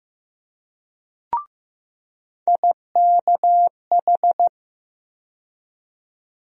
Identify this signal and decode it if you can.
Morse code, heard as a keyed tone, about 15 wpm, 700 Hz. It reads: IKH
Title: USA Call Sign Postfixes - 3 Letters 15wpm